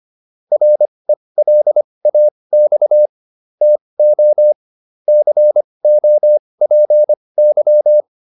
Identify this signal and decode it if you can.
Morse code, heard as a keyed tone, about 25 wpm, spaced as for 20 wpm, 600 Hz. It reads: RELAX TO COPY